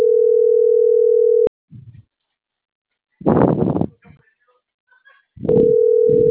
Ringback tone (North America), 440 + 480 Hz.